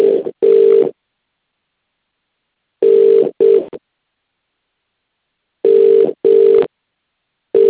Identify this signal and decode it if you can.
Ringback tone (UK), 400 + 450 Hz.